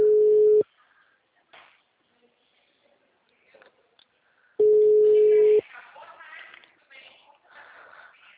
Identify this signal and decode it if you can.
Ringback tone (Europe, 425 Hz), with speech.
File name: out- -2005- - - .45704